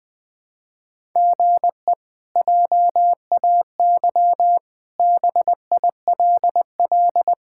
Morse code, heard as a keyed tone, about 20 wpm, 700 Hz. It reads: GE JAY BILL
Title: K1USN SST QSO Practice - 20wpm